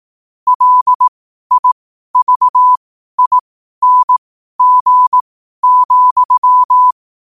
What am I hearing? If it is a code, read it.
Morse code, heard as a keyed tone, about 18 wpm, spaced as for 12 wpm, 1000 Hz. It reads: LIVING,
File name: qotdfiles / 12wpm